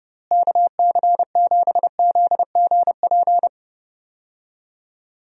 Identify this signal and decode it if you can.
Morse code, heard as a keyed tone, about 30 wpm, 700 Hz. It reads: KC7ZGP
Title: USA Call Signs - 6 Characters - Repeated 2x - 30wpm